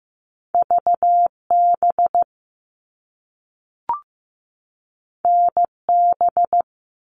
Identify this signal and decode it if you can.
Morse code, heard as a keyed tone, about 15 wpm, 700 Hz. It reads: VB NB
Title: All Cut Numbers - 2-Digit 15wpm